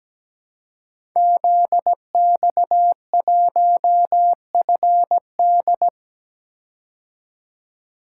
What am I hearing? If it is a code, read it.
Morse code, heard as a keyed tone, about 17 wpm, 700 Hz. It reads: ZX1FD
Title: Code Groups - 5 Letter:Numbers - 17wpm